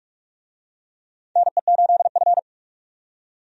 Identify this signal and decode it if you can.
Morse code, heard as a keyed tone, about 46 wpm, 700 Hz. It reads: NE8F
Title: USA Call Signs - 4 Characters - 45wpm